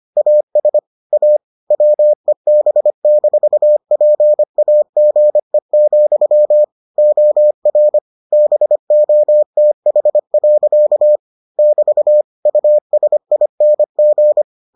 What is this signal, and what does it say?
Morse code, heard as a keyed tone, about 25 wpm, 600 Hz. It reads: AS A WEB-PAGE, OR BOTH. <BT> USING